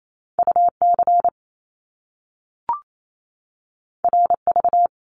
Morse code, heard as a keyed tone, about 28 wpm, 700 Hz. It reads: UC RV
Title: Top Two Letter Combinations in Words 28wpm